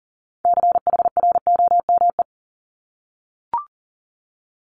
Morse code, heard as a keyed tone, about 38 wpm, 700 Hz. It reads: CHROME
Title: Multi-letter Phonemes 40wpm